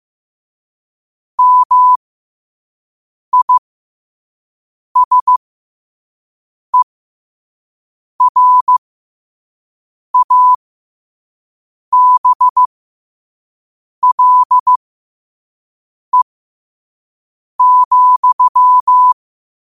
Morse code, heard as a keyed tone, about 15 wpm, spaced as for 5 wpm, 1000 Hz. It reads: MISERABLE,
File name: qotdfiles / 5wpm